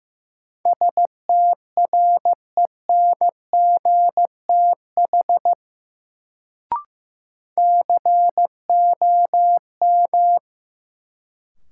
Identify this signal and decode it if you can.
Morse code, heard as a keyed tone, about 15 wpm, 700 Hz. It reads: STRENGTH COM
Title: Sentences from Top 1300 Words - New Word Drill Alt - 15wpm